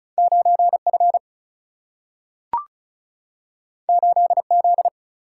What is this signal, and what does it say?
Morse code, heard as a keyed tone, about 36 wpm, 700 Hz. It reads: 9F 8Z
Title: Single-Letter Single-Number 35wpm